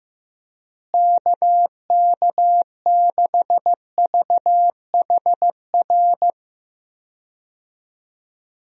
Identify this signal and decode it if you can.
Morse code, heard as a keyed tone, about 15 wpm, 700 Hz. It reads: KK6VHR
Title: USA Call Signs - 6 Characters - Repeated 3x - 15wpm